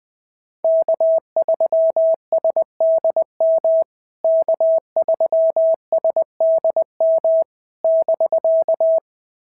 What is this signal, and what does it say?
Morse code, heard as a keyed tone, about 20 wpm, 650 Hz. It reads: K3SDM K3SDM <BK>
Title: POTA QSO Practice - 20wpm